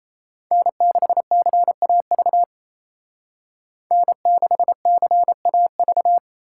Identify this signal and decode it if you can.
Morse code, heard as a keyed tone, about 36 wpm, 700 Hz. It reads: N6CAV N6CAV